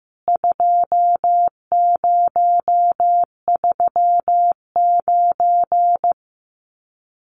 Morse code, heard as a keyed tone, about 15 wpm, 700 Hz. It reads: 2039
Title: CWT Exchange 15wpm